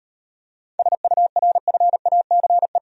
Morse code, heard as a keyed tone, about 38 wpm, 700 Hz. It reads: SURFACE